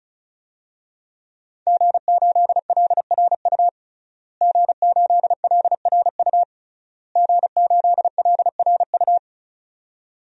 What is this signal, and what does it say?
Morse code, heard as a keyed tone, about 36 wpm, 700 Hz. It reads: G8LRU G8LRU G8LRU
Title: DXCC-100 Call Signs - 5 Characters - Repeated 3x - 35wpm